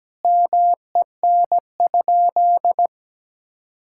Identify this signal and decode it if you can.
Morse code, heard as a keyed tone, about 17 wpm, 700 Hz. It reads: MEN?
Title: Sentences from Top 1000 Words - Review Part 6 - 17wpm